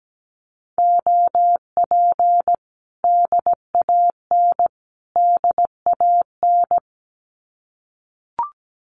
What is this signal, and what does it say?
Morse code, heard as a keyed tone, about 17 wpm, 700 Hz. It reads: OP DAN DAN
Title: Common QSO Phrases Encore 17wpm